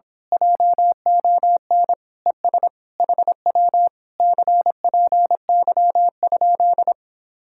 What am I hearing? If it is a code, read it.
Morse code, heard as a keyed tone, about 26 wpm, 700 Hz. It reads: JON ES HW CPY?
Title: Speed-Racing Common QSO Phrases Encore - 22wpm